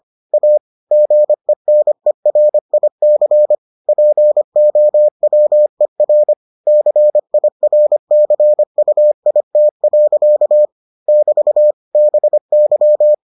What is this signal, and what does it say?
Morse code, heard as a keyed tone, about 25 wpm, 600 Hz. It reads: A GENERIC POWER CIRCUIT. <BT> BY